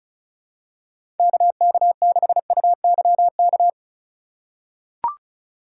Morse code, heard as a keyed tone, about 36 wpm, 700 Hz. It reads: KK6UYK
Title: USA Call Signs - 3x Character Spacing - 35wpm